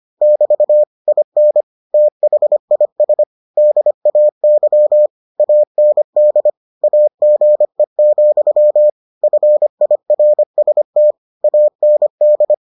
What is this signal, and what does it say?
Morse code, heard as a keyed tone, about 25 wpm, 600 Hz. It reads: <BT> IN THIS DAY AND AGE, FIRST AND